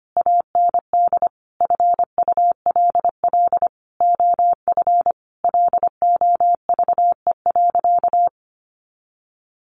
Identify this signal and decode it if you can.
Morse code, heard as a keyed tone, about 25 wpm, 700 Hz. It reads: AND FULL OF LOVE.
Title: Sentences from Top 1500 Words 25wpm